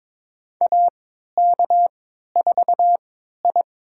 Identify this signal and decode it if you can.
Morse code, heard as a keyed tone, about 23 wpm, spaced as for 13 wpm, 700 Hz. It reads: AK4I
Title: USA Call Signs - 4 Characters - 3x Character Spacing - 22wpm